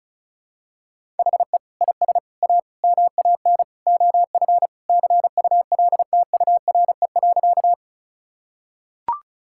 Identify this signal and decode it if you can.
Morse code, heard as a keyed tone, about 36 wpm, 700 Hz. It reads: HE IS A MAN OF CULTURE.
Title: Sentences from Top 1000 Words - Review Part 6 - 35wpm